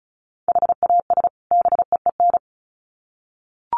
Morse code, heard as a keyed tone, about 34 wpm, 700 Hz. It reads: HAS BEEN
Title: Sets of 2 Words 35wpm